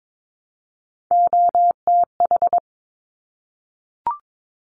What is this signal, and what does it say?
Morse code, heard as a keyed tone, about 22 wpm, 700 Hz. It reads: OTH
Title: Top Three Letter Combinations in Words 22wpm